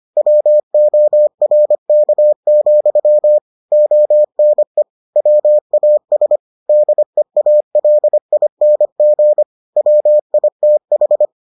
Morse code, heard as a keyed tone, about 25 wpm, 600 Hz. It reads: WORK, ONE WAS DEALING WITH